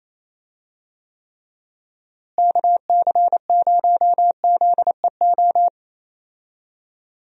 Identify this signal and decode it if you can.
Morse code, heard as a keyed tone, about 28 wpm, 700 Hz. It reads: KC0ZEO
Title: USA Call Signs - 6 Characters - Repeated 2x - 28wpm